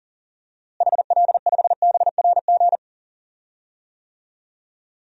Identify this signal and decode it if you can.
Morse code, heard as a keyed tone, about 40 wpm, 700 Hz. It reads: HL5BRG
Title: DXCC-100 Call Signs - 6 Characters - Repeated 3x - 40wpm